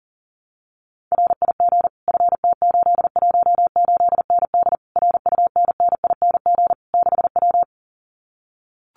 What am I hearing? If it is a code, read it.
Morse code, heard as a keyed tone, about 40 wpm, 700 Hz. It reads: RIG FT818ND RUNNING 6W
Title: Common QSO Phrases Encore 40wpm